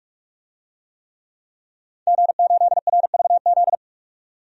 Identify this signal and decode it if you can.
Morse code, heard as a keyed tone, about 41 wpm, 700 Hz. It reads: G8RVZ
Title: DXCC-100 Call Signs - 5 Characters - 45wpm